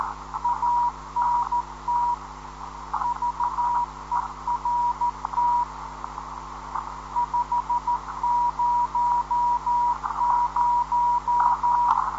Morse code, heard as a keyed tone, about 13 wpm, 960 Hz, with radio noise. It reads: ANT VERT 500